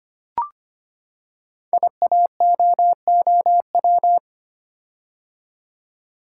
Morse code, heard as a keyed tone, about 25 wpm, 700 Hz. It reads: IAOOW
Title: Code Groups - 5 Letters - 25wpm